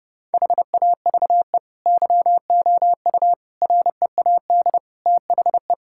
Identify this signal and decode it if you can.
Morse code, heard as a keyed tone, about 30 wpm, 700 Hz. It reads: HAVE YOU READ THE
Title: Sentences from Top 1000 Words - Review Part 5 - 30wpm